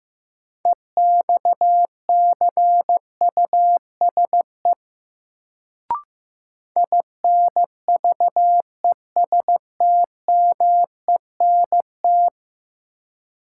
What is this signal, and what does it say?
Morse code, heard as a keyed tone, about 15 wpm, 700 Hz. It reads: EXCUSE INVESTMENT
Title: Sentences from Top 700 Words - New Word Drill - 15wpm